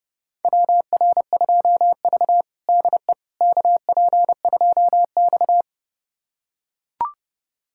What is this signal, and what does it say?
Morse code, heard as a keyed tone, about 30 wpm, 700 Hz. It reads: WR2V DE KP2X
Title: USA Callsign DE Callsign - 4 Characters - 30wpm